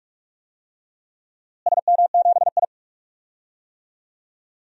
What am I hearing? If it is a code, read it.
Morse code, heard as a keyed tone, about 46 wpm, 700 Hz. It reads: SM7I
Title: DXCC-100 Call Signs - 4 Characters - Repeated 2x - 45wpm